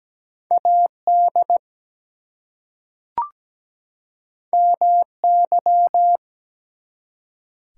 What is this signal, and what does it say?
Morse code, heard as a keyed tone, about 17 wpm, 700 Hz. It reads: AD MY
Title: Two Letter Words - 17wpm